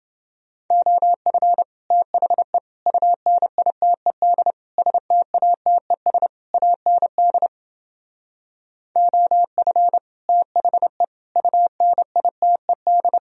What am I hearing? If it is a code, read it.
Morse code, heard as a keyed tone, about 30 wpm, 700 Hz. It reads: OF THE UNITED STATES AND OF THE UNITED